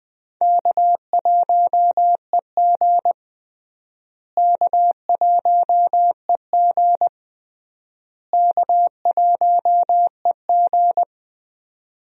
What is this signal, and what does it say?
Morse code, heard as a keyed tone, about 20 wpm, 700 Hz. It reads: K1EG K1EG K1EG
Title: USA Call Signs - 4 Characters - Repeated 3x - 20wpm